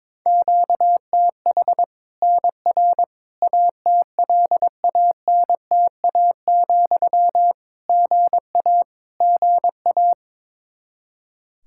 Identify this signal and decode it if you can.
Morse code, heard as a keyed tone, about 21 wpm, 700 Hz. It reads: QTH NR ATLANTA, GA GA
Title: Practice QSO - Exchange 1 - Beginning - 22wpm